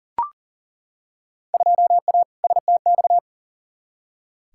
Morse code, heard as a keyed tone, about 40 wpm, 700 Hz. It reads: 2A STX
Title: Field Day Exchanges - 40wpm